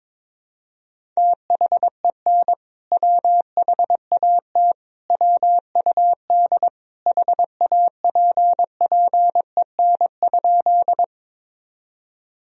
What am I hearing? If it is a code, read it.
Morse code, heard as a keyed tone, about 22 wpm, 700 Hz. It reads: THEN WHAT WUD HAPPEN?
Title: Sentences from Top 400 Words and Abbreviations 22wpm